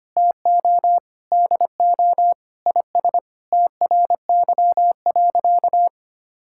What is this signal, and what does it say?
Morse code, heard as a keyed tone, about 25 wpm, 700 Hz. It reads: TO DO IS TRY.
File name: practice / 25wpm / Sentences with 3-Word Phrases - 25wpm